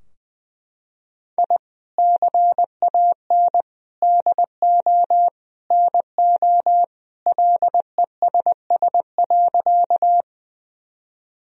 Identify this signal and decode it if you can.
Morse code, heard as a keyed tone, about 20 wpm, 700 Hz. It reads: I CAN DO NO LESS.
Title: Sentences from Top 1000 Words - Review Part 2 - 20wpm